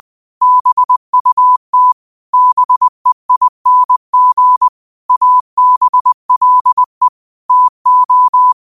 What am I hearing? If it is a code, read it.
Morse code, heard as a keyed tone, about 20 wpm, 1000 Hz. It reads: BUT BEING ABLE TO